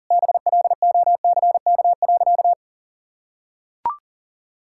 Morse code, heard as a keyed tone, about 40 wpm, 700 Hz. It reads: BLOCK.
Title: Sentences from Top 1600 Words 40wpm